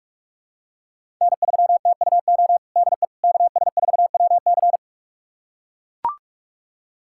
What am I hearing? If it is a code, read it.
Morse code, heard as a keyed tone, about 41 wpm, 700 Hz. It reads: N3TUO DE KS4WC
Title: USA Callsign DE Callsign - 5 Characters - 45wpm